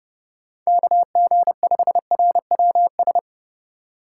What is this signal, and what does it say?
Morse code, heard as a keyed tone, about 28 wpm, 700 Hz. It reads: KG5RWS